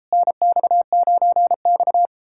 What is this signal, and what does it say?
Morse code, heard as a keyed tone, about 38 wpm, 700 Hz. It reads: NX9X